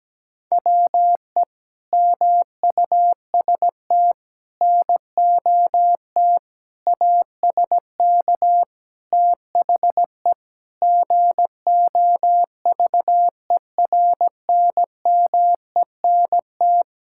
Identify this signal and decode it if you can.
Morse code, heard as a keyed tone, about 17 wpm, 700 Hz. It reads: WE MUST NOT ASK THE GOVERNMENT
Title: Sentences from Top 1600 Words 17wpm